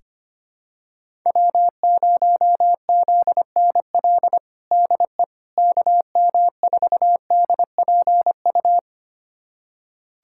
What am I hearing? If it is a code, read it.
Morse code, heard as a keyed tone, about 25 wpm, 700 Hz. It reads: W0ZNL DE KM4DPU
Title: USA Callsign DE Callsign 25wpm